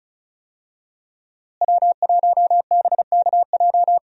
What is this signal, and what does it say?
Morse code, heard as a keyed tone, about 35 wpm, 700 Hz. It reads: W1BKJ
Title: USA Call Signs - 5 Characters - Repeated 2x - 35wpm